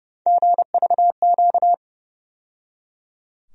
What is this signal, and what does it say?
Morse code, heard as a keyed tone, about 30 wpm, 700 Hz. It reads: GVQ
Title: USA Call Sign Postfixes - 3 Letters 30wpm